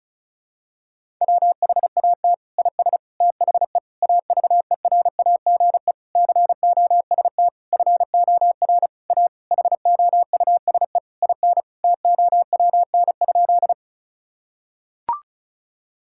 Morse code, heard as a keyed tone, about 36 wpm, 700 Hz. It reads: WHAT IS THE AVERAGE COST FOR A HOUSE IN TOWN?
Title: Sentences from Top 1000 Words - Review Part 6 - 35wpm